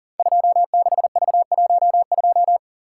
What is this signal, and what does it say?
Morse code, heard as a keyed tone, about 42 wpm, 690 Hz. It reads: 26V12
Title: Code Groups - 5 Letter:Numbers - 40wpm